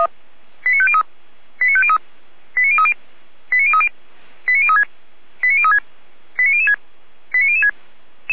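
Five-tone selective call (ZVEI), heard as quick runs of five tones, tones of about 70 ms. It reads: 89503 89503 89030 89030 89036 89036 89006 89006
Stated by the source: ZVEI five-tone call received by SDR.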